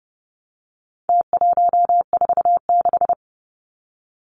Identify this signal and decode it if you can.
Morse code, heard as a keyed tone, about 30 wpm, 700 Hz. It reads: T146
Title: Numbers - Contest Style - 4-Digit 30wpm